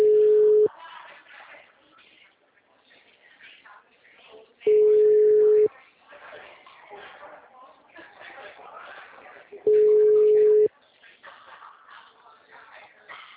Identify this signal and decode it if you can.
Ringback tone (Europe, 425 Hz), with speech.
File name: out- -2003- - - .21